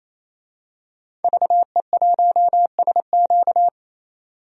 Morse code, heard as a keyed tone, about 28 wpm, 700 Hz. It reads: VE1SQ
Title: DXCC-100 Call Signs - 5 Characters - 28wpm